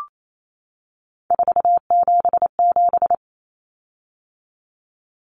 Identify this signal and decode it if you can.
Morse code, heard as a keyed tone, about 28 wpm, 700 Hz. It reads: 477
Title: Common Cut Numbers - 3-Digit 28wpm